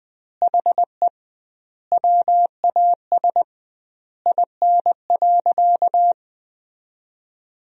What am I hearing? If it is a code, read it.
Morse code, heard as a keyed tone, about 20 wpm, 700 Hz. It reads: HE WAS IN.
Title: 2x Word-Spacing - Sentences from Top 400 Words - 20wpm